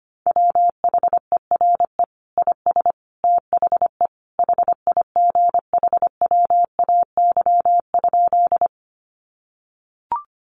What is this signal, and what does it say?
Morse code, heard as a keyed tone, about 25 wpm, 700 Hz. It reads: WHERE IS THE HIGHWAY?
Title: Sentences from Top 1400 Words 25wpm